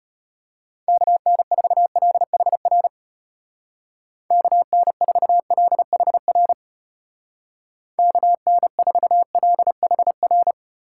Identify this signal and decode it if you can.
Morse code, heard as a keyed tone, about 34 wpm, 700 Hz. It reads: KN4LHR KN4LHR KN4LHR